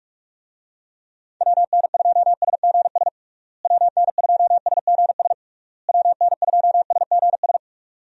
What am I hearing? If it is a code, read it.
Morse code, heard as a keyed tone, about 41 wpm, 700 Hz. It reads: WN2SGS WN2SGS WN2SGS
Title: USA Call Signs - 6 Characters - Repeated 3x - 45wpm